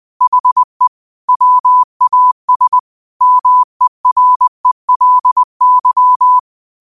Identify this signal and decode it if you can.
Morse code, heard as a keyed tone, about 20 wpm, 1000 Hz. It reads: HE WAS MERELY